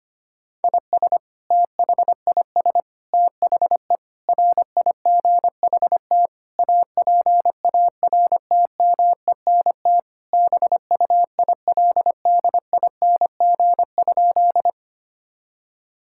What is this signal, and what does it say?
Morse code, heard as a keyed tone, about 25 wpm, 700 Hz. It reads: IS THIS THE RIGHT APARTMENT BUILDING?
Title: Sentences from Top 1000 Words - Review Part 6 - 25wpm